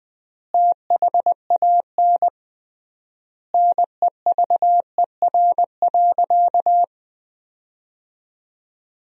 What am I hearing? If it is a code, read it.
Morse code, heard as a keyed tone, about 20 wpm, 700 Hz. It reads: THAN NEVER.
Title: 3x Word-Spacing - Sentences from Top 400 Words - 20wpm